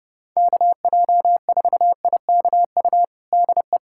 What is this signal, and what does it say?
Morse code, heard as a keyed tone, about 30 wpm, 700 Hz. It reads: KJ4IKU DE
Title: USA Callsign DE Callsign - 6 Characters - 30wpm